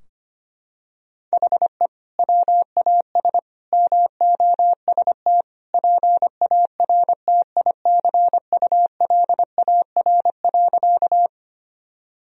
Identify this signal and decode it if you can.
Morse code, heard as a keyed tone, about 25 wpm, 700 Hz. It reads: HE WAS MOST PARTICULAR.
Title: Sentences from Top 1000 Words - Review Part 4 - 25wpm